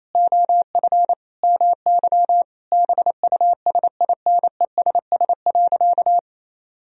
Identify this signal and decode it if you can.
Morse code, heard as a keyed tone, about 27 wpm, 700 Hz. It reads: OF MY BUSINESS.
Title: Sentences from Top 1000 Words - Review Part 2 - 28wpm